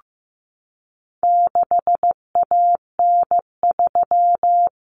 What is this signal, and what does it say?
Morse code, heard as a keyed tone, about 15 wpm, 700 Hz. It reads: 6AN3
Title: Common Cut Numbers - 4-Digit 15wpm